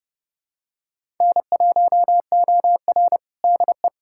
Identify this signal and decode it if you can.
Morse code, heard as a keyed tone, about 30 wpm, 700 Hz. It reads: N1OR DE
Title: USA Callsign DE Callsign - 4 Characters - 30wpm